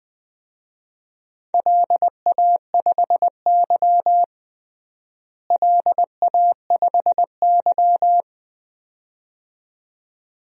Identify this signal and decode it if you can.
Morse code, heard as a keyed tone, about 20 wpm, 700 Hz. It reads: LA5Y LA5Y
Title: DXCC-100 Call Signs - 4 Characters - Repeated 2x - 20wpm